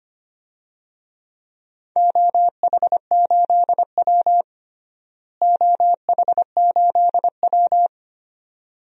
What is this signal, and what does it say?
Morse code, heard as a keyed tone, about 25 wpm, 700 Hz. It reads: OH8W OH8W
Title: DXCC-100 Call Signs - 4 Characters - Repeated 2x - 25wpm